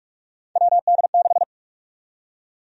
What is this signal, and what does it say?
Morse code, heard as a keyed tone, about 46 wpm, 700 Hz. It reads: WD6
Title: USA Call Sign Prefixes - Repeated 2x - 45wpm